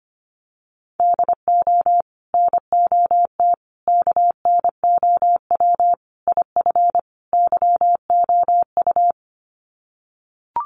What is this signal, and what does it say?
Morse code, heard as a keyed tone, about 25 wpm, 700 Hz. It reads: DO NOT KNOW IF YOU
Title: Sets of 5 Words 25wpm